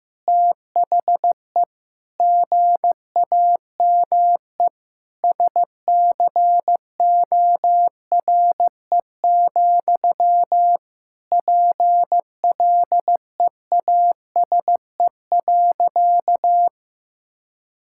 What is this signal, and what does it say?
Morse code, heard as a keyed tone, about 15 wpm, 700 Hz. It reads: THE GAME SCORE, PLEASE.
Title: Sentences from Top 1600 Words 15wpm